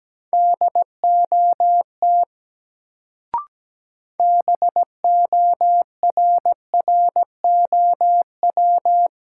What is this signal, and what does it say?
Morse code, heard as a keyed tone, about 17 wpm, 700 Hz. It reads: DOT BORROW
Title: Sentences from Top 1500 Words - New Word Drill - 17wpm